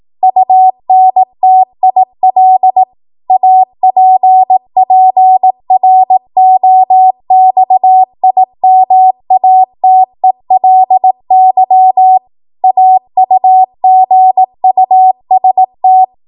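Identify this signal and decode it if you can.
Morse code, heard as a keyed tone, about 18 wpm, 750 Hz. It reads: UNTIL APPROXIMATELY AUGUST
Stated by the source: ARRL 18 wpm Morse code practice archive transmission.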